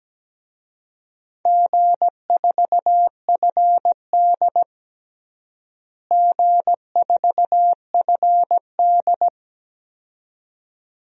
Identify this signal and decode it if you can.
Morse code, heard as a keyed tone, about 17 wpm, 700 Hz. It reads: G4FD G4FD
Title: DXCC-100 Call Signs - 4 Characters - Repeated 2x - 17wpm